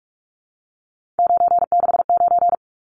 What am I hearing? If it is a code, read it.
Morse code, heard as a keyed tone, about 44 wpm, 700 Hz. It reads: 969
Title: Numbers - 3-Digit 45wpm